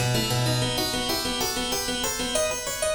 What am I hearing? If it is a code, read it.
There is no code.